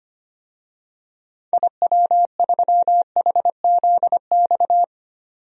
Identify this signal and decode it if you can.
Morse code, heard as a keyed tone, about 25 wpm, 700 Hz. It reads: IW3HZX
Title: DXCC-100 Call Signs - 6 Characters - 25wpm